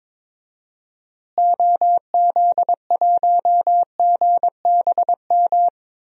Morse code, heard as a keyed tone, about 21 wpm, 700 Hz. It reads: OZ1GBM
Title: DXCC-100 Call Signs - 6 Characters - Repeated 3x - 22wpm